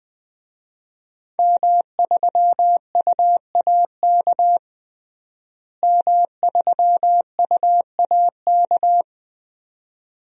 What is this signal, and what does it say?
Morse code, heard as a keyed tone, about 21 wpm, 700 Hz. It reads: M3UAK M3UAK
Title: DXCC-100 Call Signs - 5 Characters - Repeated 2x - 20wpm